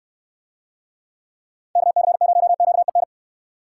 Morse code, heard as a keyed tone, about 69 wpm, 700 Hz. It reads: KC0QEN